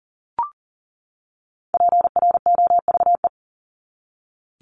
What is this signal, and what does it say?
Morse code, heard as a keyed tone, about 38 wpm, 700 Hz. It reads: PROVE
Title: ED Suffix - Top 100 Words with Base Words 40wpm